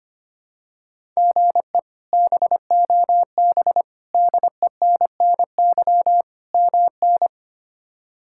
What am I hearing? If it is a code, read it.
Morse code, heard as a keyed tone, about 25 wpm, 700 Hz. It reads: GE BOB DENNY MN
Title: K1USN SST QSO Practice - 25wpm